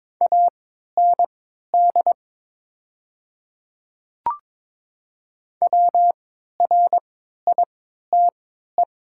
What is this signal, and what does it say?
Morse code, heard as a keyed tone, about 22 wpm, spaced as for 13 wpm, 700 Hz. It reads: AND WRITE
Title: 3x Character Spacing - Top 100 Words - 22wpm